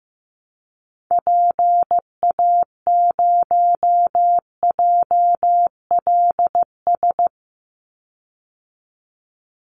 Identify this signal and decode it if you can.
Morse code, heard as a keyed tone, about 15 wpm, 700 Hz. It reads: PA0JLS
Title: Call Signs Repeated 3x - 15wpm